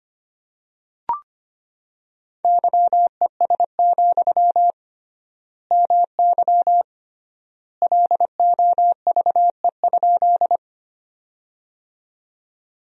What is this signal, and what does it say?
Morse code, heard as a keyed tone, about 25 wpm, 700 Hz. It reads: YES, MY LOVE?
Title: 3x Word-Spacing - Sentences from Top 500 Words - 25wpm